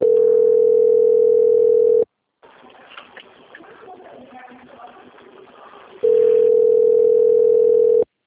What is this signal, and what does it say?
Ringback tone (North America), 440 + 480 Hz.
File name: out- - - - - .7121